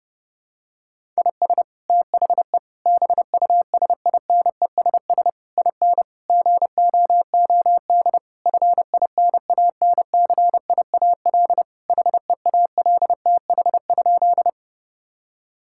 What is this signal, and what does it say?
Morse code, heard as a keyed tone, about 30 wpm, 700 Hz. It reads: IS THE BUSINESS IN GOOD FINANCIAL HEALTH?
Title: Sentences from Top 1000 Words - Review Part 5 - 30wpm